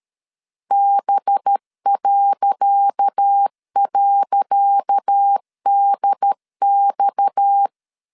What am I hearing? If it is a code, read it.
Morse code, heard as a keyed tone, about 13 wpm, 800 Hz. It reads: B..DX
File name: LEZIONI / 10 SIGNS / 10D COMA 12wpm 2544